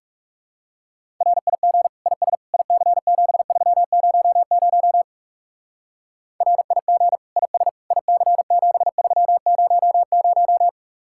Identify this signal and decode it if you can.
Morse code, heard as a keyed tone, about 38 wpm, 700 Hz. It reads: RIG IS IC7300 RIG IS IC7300